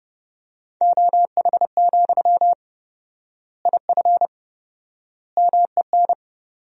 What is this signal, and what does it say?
Morse code, heard as a keyed tone, about 30 wpm, 700 Hz. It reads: OH, IF MEN
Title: 4x Word-Spacing - Sentences from Top 500 Words - 30wpm